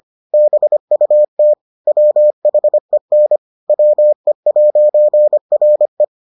Morse code, heard as a keyed tone, about 25 wpm, 600 Hz. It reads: BUT WHEN WE'RE